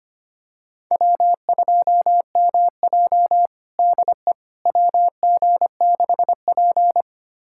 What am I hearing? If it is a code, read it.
Morse code, heard as a keyed tone, about 25 wpm, 700 Hz. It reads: W2MJ DE WG6P
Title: USA Callsign DE Callsign - 4 Characters - 25wpm